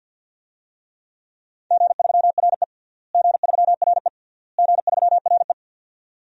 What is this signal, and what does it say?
Morse code, heard as a keyed tone, about 51 wpm, 700 Hz. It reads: G3RE G3RE G3RE
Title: DXCC-100 Call Signs - 4 Characters - Repeated 3x - 50wpm